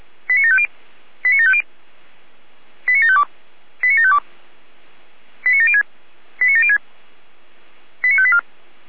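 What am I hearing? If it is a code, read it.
Five-tone selective call (ZVEI), heard as quick runs of five tones, tones of about 70 ms. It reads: 89750 89750 89752 89752 89706 89706 89584
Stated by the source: ZVEI five-tone selective call 89581 received on an SDR.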